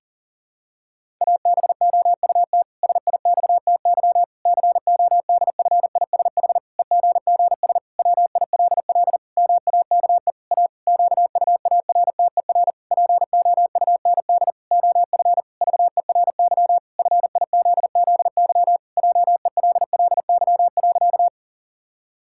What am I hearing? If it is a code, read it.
Morse code, heard as a keyed tone, about 37 wpm, 700 Hz. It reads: ABOUT SIXTY CODFISH EGGS WILL MAKE A QUARTER POUND OF VERY FIZZY JELLY.
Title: Pangram Challenge v2 - 40wpm